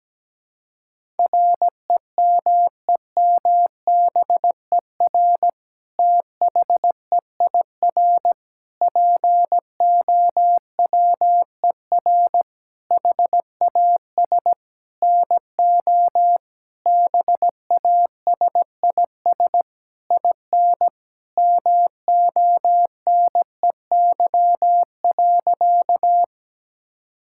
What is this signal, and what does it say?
Morse code, heard as a keyed tone, about 17 wpm, 700 Hz. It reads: REMEMBER THEIR POWER HAS NO BASIS IN MONEY.
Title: Sentences from Top 1000 Words - Review Part 5 - 17wpm